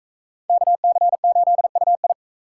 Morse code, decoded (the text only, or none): KC8UI